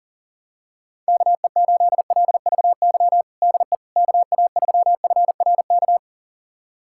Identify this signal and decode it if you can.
Morse code, heard as a keyed tone, about 40 wpm, 700 Hz. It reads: KE8LVY DE KA3FRK
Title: USA Callsign DE Callsign - 6 Characters - 40wpm